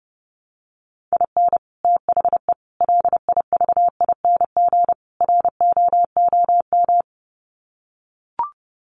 Morse code, heard as a keyed tone, about 30 wpm, 700 Hz. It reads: IN THE LIVING ROOM
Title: Sets of 4 Words 30wpm